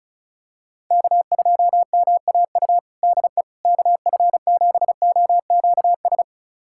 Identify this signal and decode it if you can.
Morse code, heard as a keyed tone, about 36 wpm, 700 Hz. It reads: K2MAU DE KF7OQS
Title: USA Callsign DE Callsign 35wpm